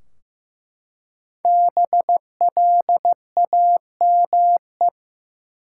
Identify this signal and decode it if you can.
Morse code, heard as a keyed tone, about 15 wpm, 700 Hz. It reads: BLAME
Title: Five Letter Words - 15wpm